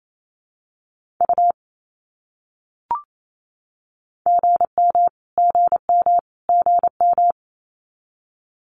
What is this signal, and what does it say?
Morse code, heard as a keyed tone, about 28 wpm, 700 Hz. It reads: U GM GM GM
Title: Abbreviations Repeated 3x - 28wpm